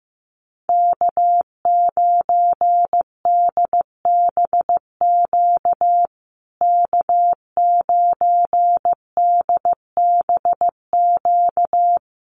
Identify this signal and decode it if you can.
Morse code, heard as a keyed tone, about 15 wpm, 700 Hz. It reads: K9DBQ K9DBQ
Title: USA Call Signs Repeated 3x 15wpm